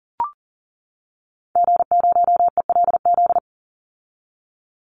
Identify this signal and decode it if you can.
Morse code, heard as a keyed tone, about 40 wpm, 700 Hz. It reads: G0ELZ
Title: Call Signs - 40wpm